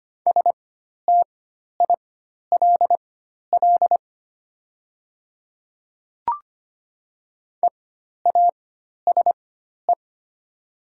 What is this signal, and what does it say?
Morse code, heard as a keyed tone, about 25 wpm, spaced as for 12 wpm, 700 Hz. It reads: STILL EASE
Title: 4x Character Spacing - Top 300 Words - 25wpm